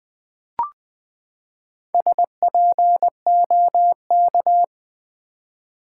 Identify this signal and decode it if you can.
Morse code, heard as a keyed tone, about 20 wpm, 700 Hz. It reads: SPOK
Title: Build A Word - 4 Letters At A Time - 12-Letter Words - 20wpm